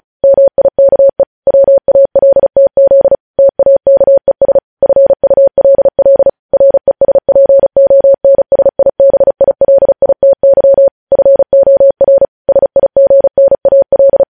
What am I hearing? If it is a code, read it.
Morse code, heard as a keyed tone, about 36 wpm, 550 Hz. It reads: MIKE WALTZ TAKES FULL RESPONSIBILITY FOR SIGNAL
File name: podcast / News headlines-1x-35x35